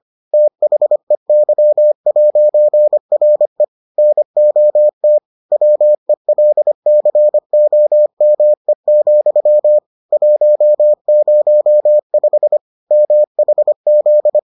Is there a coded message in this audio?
Yes: Morse code from 0.3 to 14.4 s, heard as a keyed tone, about 25 wpm, 600 Hz. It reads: THEY'RE NOT WELCOME, 105 MHZ